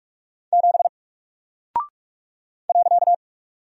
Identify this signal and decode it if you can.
Morse code, heard as a keyed tone, about 44 wpm, 700 Hz. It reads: 7 .
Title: Single Character (Letter:Number:Punctuation:Prosign) Alt 45wpm